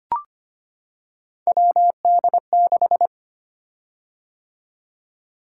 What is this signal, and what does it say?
Morse code, heard as a keyed tone, about 25 wpm, 700 Hz. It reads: WD6
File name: Three Letter-Number Combinations - 25wpm